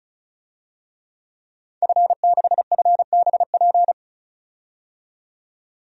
Morse code, heard as a keyed tone, about 34 wpm, 700 Hz. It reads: F6FBP
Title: DXCC-100 Call Signs - 5 Characters - 35wpm